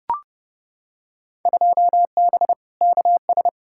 Morse code, heard as a keyed tone, about 30 wpm, 700 Hz. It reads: 2B KS
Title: Field Day Exchanges Encore - 30wpm